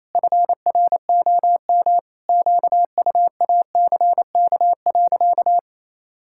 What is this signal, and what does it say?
Morse code, heard as a keyed tone, about 28 wpm, 700 Hz. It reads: FROM QUACK.
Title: Pangram Challenge v2 - 28wpm